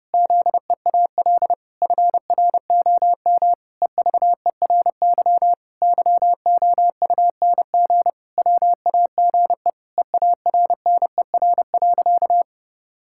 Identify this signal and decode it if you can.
Morse code, heard as a keyed tone, about 32 wpm, 700 Hz. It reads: ZEAL FROM EVERY YOUNG WAGE EARNER.
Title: Pangram Challenge v2 - 30wpm